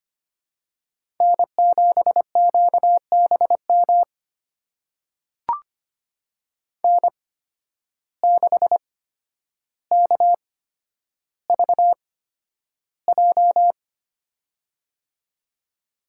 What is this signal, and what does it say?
Morse code, heard as a keyed tone, about 25 wpm, spaced as for 11 wpm, 700 Hz. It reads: N7QBM N6KVJ
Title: USA Call Signs - 5 Characters - 8x Character Spacing - 25wpm